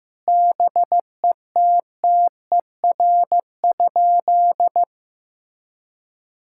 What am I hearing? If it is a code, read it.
Morse code, heard as a keyed tone, about 15 wpm, 700 Hz. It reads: BETTER?
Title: Sentences from Top 1000 Words - Review Part 6 - 15wpm